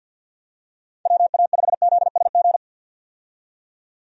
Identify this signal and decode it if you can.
Morse code, heard as a keyed tone, about 53 wpm, 700 Hz. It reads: WA5ZSG
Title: USA Call Signs - 6 Characters - 50wpm